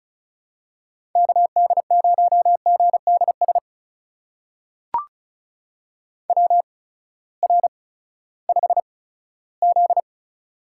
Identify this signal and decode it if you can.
Morse code, heard as a keyed tone, about 36 wpm, spaced as for 18 wpm, 700 Hz. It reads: KD0GDS WR5Z
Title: USA Call Signs - 8x Character Spacing - 35wpm